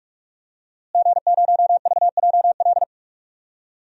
Morse code, heard as a keyed tone, about 43 wpm, 700 Hz. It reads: G0VJL
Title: DXCC-100 Call Signs - 5 Characters - Repeated 3x - 45wpm